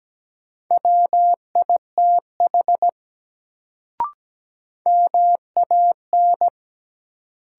Morse code, heard as a keyed tone, about 17 wpm, 700 Hz. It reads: WITH MAN
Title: Sets of 2 Words - New Word Drill - 17wpm